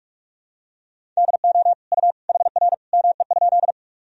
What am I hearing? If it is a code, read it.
Morse code, heard as a keyed tone, about 47 wpm, 700 Hz. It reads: DO U HR ME?
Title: Sentences from Top 300 Words and Abbreviations 45wpm